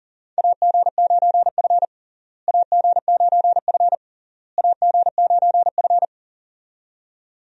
Morse code, heard as a keyed tone, about 40 wpm, 700 Hz. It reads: AG9F AG9F AG9F
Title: USA Call Signs - 4 Characters - Repeated 3x - 40wpm